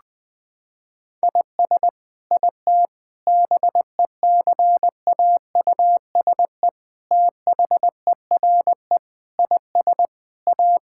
Morse code, heard as a keyed tone, about 21 wpm, 700 Hz. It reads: IS IT BECAUSE THERE IS A
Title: Sentences from Top 1000 Words - Review Part 5 - 20wpm